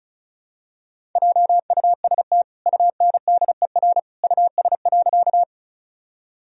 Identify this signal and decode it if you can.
Morse code, heard as a keyed tone, about 34 wpm, 700 Hz. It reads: JUST UNDER US.
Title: Sentences from Top 1000 Words - Review Part 1 - 35wpm